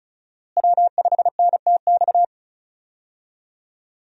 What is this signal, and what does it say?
Morse code, heard as a keyed tone, about 34 wpm, 700 Hz. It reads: W5NTX
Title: USA Call Signs - 5 Characters - Repeated 2x - 35wpm